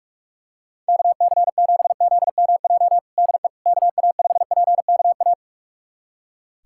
Morse code, heard as a keyed tone, about 46 wpm, 700 Hz. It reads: KC7ZMJ DE KA5PKA
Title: USA Callsign DE Callsign - 6 Characters - 45wpm